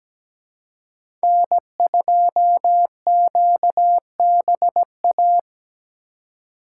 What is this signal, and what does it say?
Morse code, heard as a keyed tone, about 20 wpm, 700 Hz. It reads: N2QBA
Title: Speed Racing - USA Call Signs 17wpm